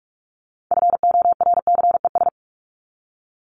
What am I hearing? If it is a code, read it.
Morse code, heard as a keyed tone, about 44 wpm, 700 Hz. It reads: FORCES
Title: ES:S Suffix - Top 100 Words with Base Words 45wpm